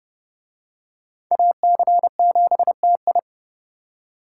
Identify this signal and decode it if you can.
Morse code, heard as a keyed tone, about 30 wpm, 700 Hz. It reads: AC7TI